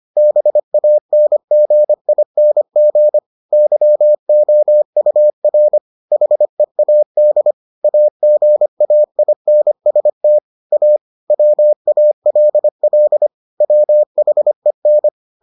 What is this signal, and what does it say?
Morse code, heard as a keyed tone, about 25 wpm, 600 Hz. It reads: BANGING YOUR HEAD AGAINST A WALL WHEN